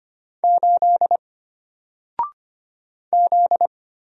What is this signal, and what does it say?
Morse code, heard as a keyed tone, about 25 wpm, 700 Hz. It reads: 8 Z
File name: Single-Letter-Number-25wpm